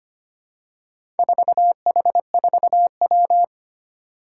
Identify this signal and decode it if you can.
Morse code, heard as a keyed tone, about 25 wpm, 700 Hz. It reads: 4H4W
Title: Four Challenging Letter-Number Combinations 25wpm